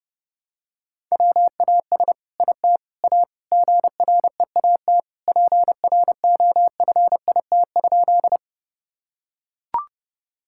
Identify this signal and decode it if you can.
Morse code, heard as a keyed tone, about 29 wpm, 700 Hz. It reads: WAS IT A GREAT PROFIT?
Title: Sentences from Top 1000 Words - Review Part 5 - 30wpm